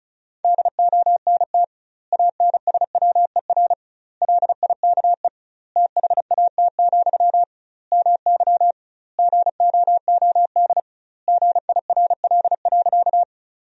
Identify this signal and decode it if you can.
Morse code, heard as a keyed tone, about 36 wpm, 700 Hz. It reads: DONT ANSWER LIKE THAT, MY GOOD GIRL.
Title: 2x Word-Spacing - Sentences from Top 400 Words - 35wpm